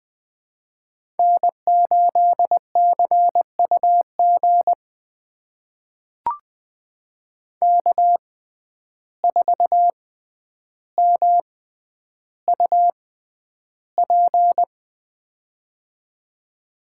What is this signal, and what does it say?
Morse code, heard as a keyed tone, about 20 wpm, spaced as for 10 wpm, 700 Hz. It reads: N8CUG K4MUP
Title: USA Call Signs - 5 Characters - 6x Character Spacing - 20wpm